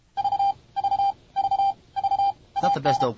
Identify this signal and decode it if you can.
Morse code, heard as a keyed tone, about 33 wpm, 790 Hz, with speech. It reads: V V V V AN